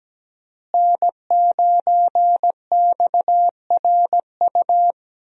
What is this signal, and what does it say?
Morse code, heard as a keyed tone, about 17 wpm, 700 Hz. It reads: N9XRU
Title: USA Call Signs - 5 Characters - Repeated 3x - 17wpm